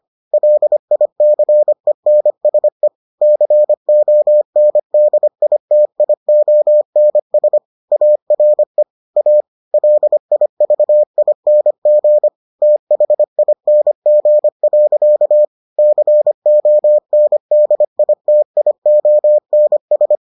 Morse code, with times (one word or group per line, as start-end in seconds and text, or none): 0.3-2.9 LICENSE
3.2-7.6 CONDITIONS
7.9-8.8 ARE
9.2-9.4 A
9.7-12.3 LIVING
12.6-15.4 THING.
15.8-20.2 CONDITIONS